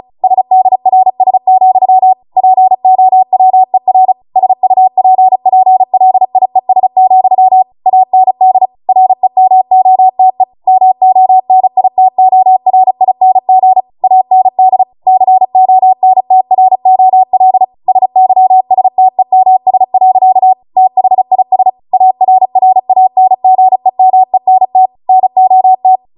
Morse code, heard as a keyed tone, about 36 wpm, 750 Hz. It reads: SDRS, POWER SUPPLIES, AND REMOTE MONITORING AND CONTROL SYSTEMS. THIS ARRANGEMENT NOT